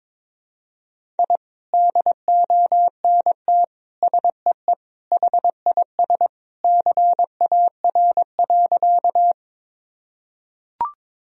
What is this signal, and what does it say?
Morse code, heard as a keyed tone, about 21 wpm, 700 Hz. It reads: I DONT SEE HIS CAR.
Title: Sentences from Top 1000 Words - Review Part 1 - 22wpm